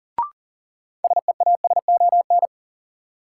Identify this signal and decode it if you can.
Morse code, heard as a keyed tone, about 40 wpm, 700 Hz. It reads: SEASON
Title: Sentences from Top 1400 Words - New Word Drill Alt - 40wpm